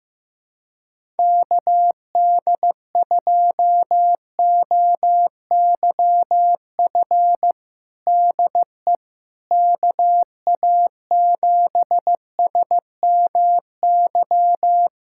Morse code, heard as a keyed tone, about 15 wpm, 700 Hz. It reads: KD2OYF DE KA7SMY
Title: USA Callsign DE Callsign - 6 Characters - 15wpm